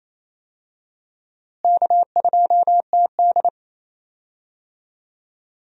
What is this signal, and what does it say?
Morse code, heard as a keyed tone, about 28 wpm, 700 Hz. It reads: K2TD
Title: USA Call Signs - 4 Characters - Repeated 2x - 28wpm